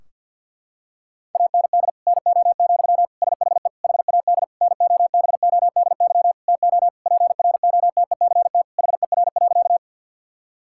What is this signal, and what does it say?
Morse code, heard as a keyed tone, about 46 wpm, 700 Hz. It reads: AND NO, SHE HAD NOBODY TO PROTECT HER.
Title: Sentences from Top 1600 Words 50wpm